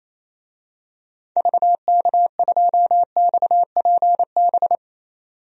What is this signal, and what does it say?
Morse code, heard as a keyed tone, about 28 wpm, 700 Hz. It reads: VK2XPB
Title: DXCC-100 Call Signs - 6 Characters - 28wpm